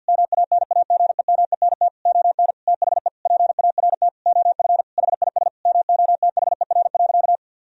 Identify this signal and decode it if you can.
Morse code, heard as a keyed tone, about 50 wpm, 700 Hz. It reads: MANAGEMENT ON THE PART OF HIS MOTHER.
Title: Sentences from Top 1000 Words - Review Part 4 - 50wpm